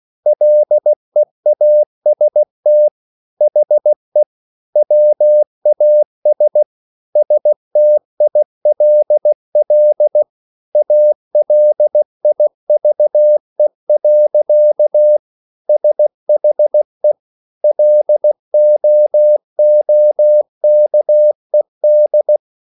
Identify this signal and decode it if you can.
Morse code, heard as a keyed tone, about 16 wpm, 600 Hz. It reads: LEAST HE WAS STILL ALIVE. SHE LOOKED